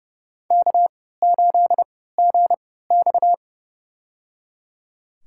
Morse code, heard as a keyed tone, about 32 wpm, spaced as for 17 wpm, 700 Hz. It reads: K8GX
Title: USA Call Signs - 4 Characters - 3x Character Spacing - 30wpm